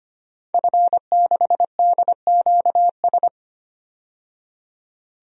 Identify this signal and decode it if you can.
Morse code, heard as a keyed tone, about 25 wpm, 700 Hz. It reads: F6DQS
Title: DXCC-100 Call Signs - 5 Characters - 25wpm